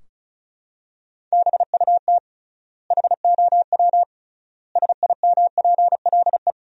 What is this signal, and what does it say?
Morse code, heard as a keyed tone, about 36 wpm, 700 Hz. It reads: BUT HOW SIMPLE
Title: 3x Word-Spacing - Sentences from Top 400 Words - 35wpm